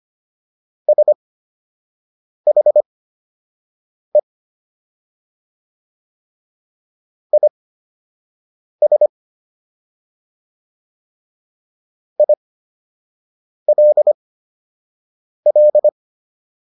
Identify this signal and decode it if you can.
Morse code, heard as a keyed tone, about 25 wpm, spaced as for 6 wpm, 600 Hz. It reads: SHE IS ILL